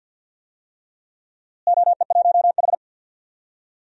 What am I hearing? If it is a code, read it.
Morse code, heard as a keyed tone, about 52 wpm, 700 Hz. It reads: OE1H
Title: DXCC-100 Call Signs - 4 Characters - Repeated 3x - 50wpm